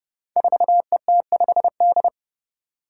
Morse code, heard as a keyed tone, about 30 wpm, 700 Hz. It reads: 4ET5D
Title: Five Challenging Letter-Number Combinations 30wpm